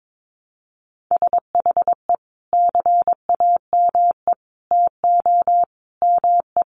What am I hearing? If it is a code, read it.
Morse code, heard as a keyed tone, about 22 wpm, 700 Hz. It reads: SHE CAME TO ME
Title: Sentences from Top 1100 Words 22wpm